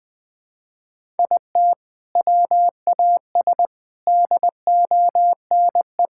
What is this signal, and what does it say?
Morse code, heard as a keyed tone, about 20 wpm, 700 Hz. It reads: IT WAS DONE
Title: Sentences from Top 1600 Words 20wpm